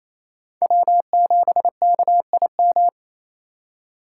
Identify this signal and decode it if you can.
Morse code, heard as a keyed tone, about 28 wpm, 700 Hz. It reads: W7KIM
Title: USA Call Signs - 5 Characters - 6x Character Spacing - 28wpm